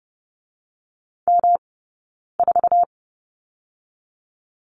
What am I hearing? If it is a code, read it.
Morse code, heard as a keyed tone, about 30 wpm, 700 Hz. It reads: M 4
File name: Lesson 18d - Letter m - Two Characters 30wpm